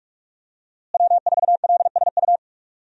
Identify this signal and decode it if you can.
Morse code, heard as a keyed tone, about 46 wpm, 700 Hz. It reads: W4LSU
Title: USA Call Signs - 5 Characters - Repeated 3x - 45wpm